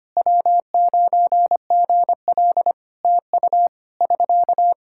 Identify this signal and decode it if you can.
Morse code, heard as a keyed tone, about 25 wpm, 700 Hz. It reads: W9GL TU <SK>
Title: Practice QSO - Exchange 4 - Ending - 25wpm